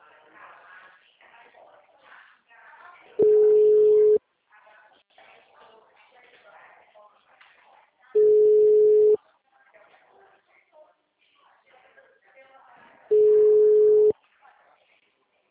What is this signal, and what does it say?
Ringback tone (Europe, 425 Hz), with speech.